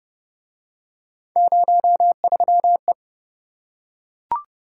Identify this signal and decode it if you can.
Morse code, heard as a keyed tone, about 26 wpm, 700 Hz. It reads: 03E